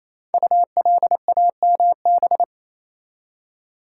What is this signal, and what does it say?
Morse code, heard as a keyed tone, about 27 wpm, 700 Hz. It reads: ULAMB
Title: Code Groups - 5 Letters - 28wpm